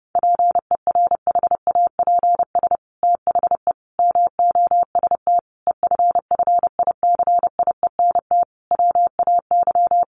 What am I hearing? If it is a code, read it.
Morse code, heard as a keyed tone, about 30 wpm, 700 Hz. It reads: PERHAPS THE MOST EFFICIENT WAY
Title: Sentences from Top 1300 Words 30wpm